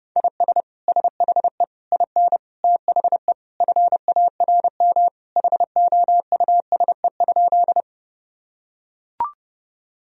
Morse code, heard as a keyed tone, about 30 wpm, 700 Hz. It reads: IS SHE IN THE FARM HOUSE?
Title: Sentences from Top 1000 Words - Review Part 2 - 30wpm